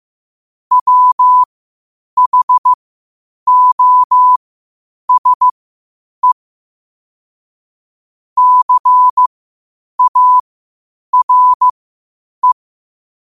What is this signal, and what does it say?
Morse code, heard as a keyed tone, about 15 wpm, spaced as for 8 wpm, 1000 Hz. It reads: WHOSE CARE
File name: qotdfiles / 8wpm